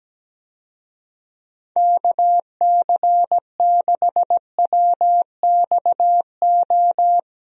Morse code, heard as a keyed tone, about 17 wpm, 700 Hz. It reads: KC6WXO
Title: USA Call Signs - 6 Characters - Repeated 2x - 17wpm